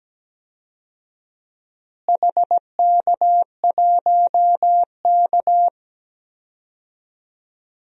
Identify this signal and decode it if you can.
Morse code, heard as a keyed tone, about 17 wpm, 700 Hz. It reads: HK1K
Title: DXCC-100 Call Signs - 4 Characters - Repeated 3x - 17wpm